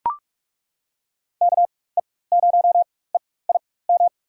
Morse code, heard as a keyed tone, about 42 wpm, spaced as for 20 wpm, 700 Hz. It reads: KE0EIM
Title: USA Call Signs - 6 Characters - 4x Character Spacing - 45wpm